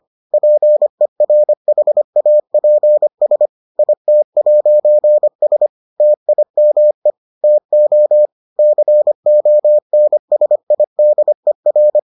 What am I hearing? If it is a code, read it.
Morse code, heard as a keyed tone, about 25 wpm, 600 Hz. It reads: PERHAPS IT'S TIME TO CONSIDER